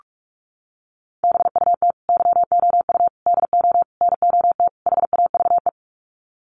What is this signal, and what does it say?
Morse code, heard as a keyed tone, about 44 wpm, 700 Hz. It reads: BUT YOU DO NOT HAVE
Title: Sets of 5 Words Encore 45wpm